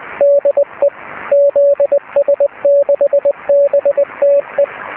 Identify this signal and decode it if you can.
Morse code, heard as a keyed tone, about 20 wpm, 570 Hz. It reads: DE ZS6BTE
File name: zs6bte